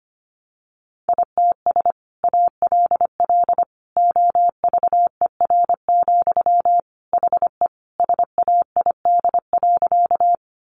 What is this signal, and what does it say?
Morse code, heard as a keyed tone, about 25 wpm, 700 Hz. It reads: ITS ALL OVER, HE SAID.
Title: Sentences from Top 600 Words 25wpm